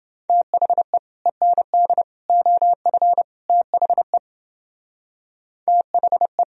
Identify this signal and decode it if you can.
Morse code, heard as a keyed tone, about 29 wpm, 700 Hz. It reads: THE END OF THE THE